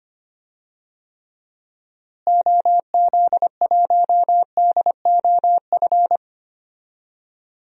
Morse code, heard as a keyed tone, about 25 wpm, 700 Hz. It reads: OZ1DOF
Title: DXCC-100 Call Signs - 6 Characters - Repeated 2x - 25wpm